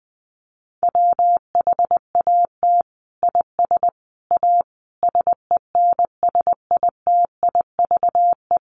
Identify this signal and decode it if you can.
Morse code, heard as a keyed tone, about 20 wpm, 700 Hz. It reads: WHAT IS A SENSITIVE
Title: Sentences from Top 1400 Words 20wpm